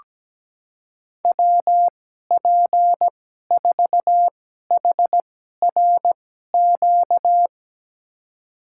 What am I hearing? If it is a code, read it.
Morse code, heard as a keyed tone, about 17 wpm, spaced as for 12 wpm, 700 Hz. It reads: WP4HRQ
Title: USA Call Signs - 6 Characters - 2x Character Spacing - 17wpm